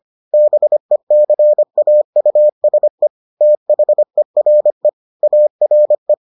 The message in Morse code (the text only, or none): BECAUSE THERE ARE